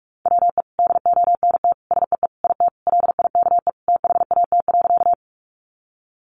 Morse code, heard as a keyed tone, about 44 wpm, 700 Hz. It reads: WE DONT SEE IT LIKE THAT.
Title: Sentences from Top 300 Words 45wpm